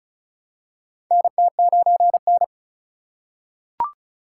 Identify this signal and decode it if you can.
Morse code, heard as a keyed tone, about 36 wpm, 690 Hz. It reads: NT9N
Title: USA Call Signs - 4 Characters - 2x Character Spacing - 35wpm